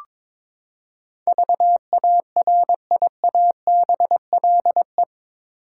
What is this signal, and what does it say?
Morse code, heard as a keyed tone, about 22 wpm, 700 Hz. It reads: VARIABLE
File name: practice / 22wpm / Eight Letter Words - 22wpm